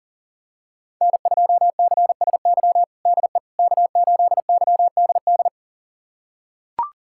Morse code, heard as a keyed tone, about 37 wpm, 700 Hz. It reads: N2CSY DE K8YDD
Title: USA Callsign DE Callsign - 5 Characters - 40wpm